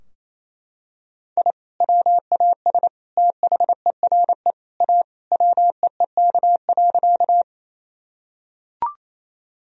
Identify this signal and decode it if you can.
Morse code, heard as a keyed tone, about 27 wpm, 700 Hz. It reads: I WAS THERE A WEEK.
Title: Sentences from Top 1000 Words - Review Part 2 - 28wpm